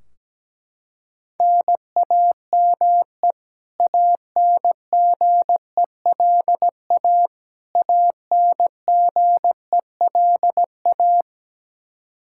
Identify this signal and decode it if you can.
Morse code, heard as a keyed tone, about 17 wpm, 700 Hz. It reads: NAME ANGELA ANGELA
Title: Practice QSO - Exchange 1 - Beginning - 17wpm